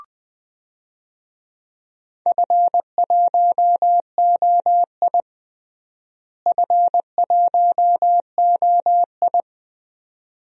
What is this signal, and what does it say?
Morse code, heard as a keyed tone, about 20 wpm, 700 Hz. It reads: F1OI F1OI
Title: DXCC-100 Call Signs - 4 Characters - Repeated 2x - 20wpm